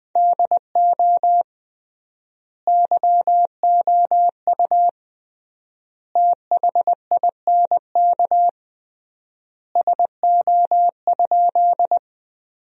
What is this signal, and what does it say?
Morse code, heard as a keyed tone, about 20 wpm, 700 Hz. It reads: DO YOU THINK SO?
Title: 3x Word-Spacing - Sentences from Top 200 Words - 20wpm